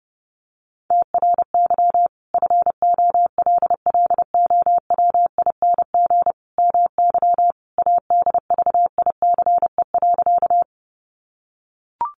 Morse code, heard as a keyed tone, about 30 wpm, 700 Hz. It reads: TRY FOLLOWING MY ADVICE.
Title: Sentences from Top 1100 Words 30wpm